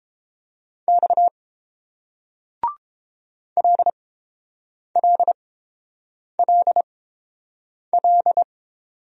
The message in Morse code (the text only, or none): X LLLL